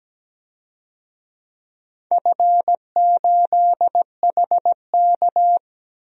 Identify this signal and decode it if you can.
Morse code, heard as a keyed tone, about 17 wpm, 700 Hz. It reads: F8HK
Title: DXCC-100 Call Signs - 4 Characters - 17wpm